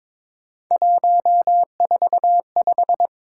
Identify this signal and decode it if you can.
Morse code, heard as a keyed tone, about 23 wpm, 700 Hz. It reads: 145
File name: practice / 22wpm / Three Letter-Number Combinations - 22wpm